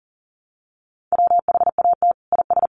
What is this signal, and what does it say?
Morse code, heard as a keyed tone, about 40 wpm, 700 Hz. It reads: WHAT IS
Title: Sentences from Top 1500 Words 40wpm